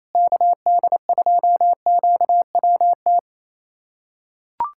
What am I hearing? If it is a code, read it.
Morse code, heard as a keyed tone, about 28 wpm, 700 Hz. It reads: KD2QWT
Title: USA Callsign DE Callsign - 6 Characters - 28wpm